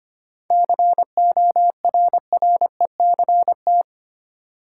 Morse code, heard as a keyed tone, about 25 wpm, 700 Hz. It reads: CORRECT
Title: Seven Letter Words - 25wpm